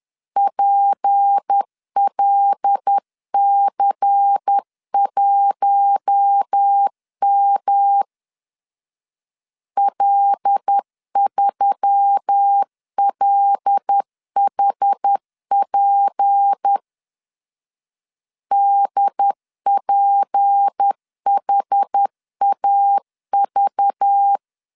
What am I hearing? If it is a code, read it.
Morse code, heard as a keyed tone, about 10 wpm, 800 Hz. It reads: PLC1M L3LHP DPHAV
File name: LEZIONI / 6 PKX0 / 6 P 10wpm 2432